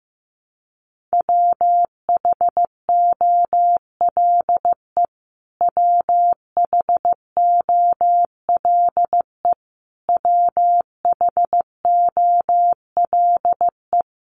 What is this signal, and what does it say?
Morse code, heard as a keyed tone, about 15 wpm, 700 Hz. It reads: WHOLE WHOLE WHOLE
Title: Top 500 Words Repeated 15wpm